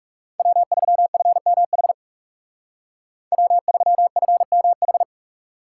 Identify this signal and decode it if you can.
Morse code, heard as a keyed tone, about 39 wpm, 700 Hz. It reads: W3FMH W3FMH